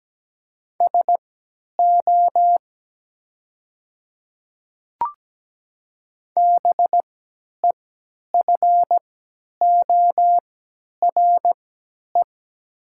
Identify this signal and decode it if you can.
Morse code, heard as a keyed tone, about 17 wpm, spaced as for 10 wpm, 700 Hz. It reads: SO BEFORE
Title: 3x Character Spacing - Top 200 Words - 17wpm